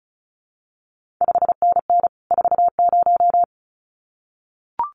Morse code, heard as a keyed tone, about 36 wpm, 700 Hz. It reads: 5NN 40
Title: CQ WW DX Contest Exchange Encore - 35wpm